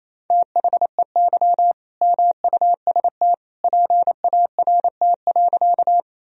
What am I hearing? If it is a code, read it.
Morse code, heard as a keyed tone, about 28 wpm, 700 Hz. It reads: THEY MUST PART.
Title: Sentences from Top 1000 Words - Review Part 1 - 28wpm